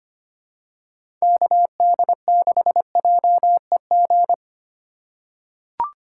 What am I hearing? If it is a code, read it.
Morse code, heard as a keyed tone, about 25 wpm, 700 Hz. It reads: KD6JEG
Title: USA Call Signs - 3x Character Spacing - 25wpm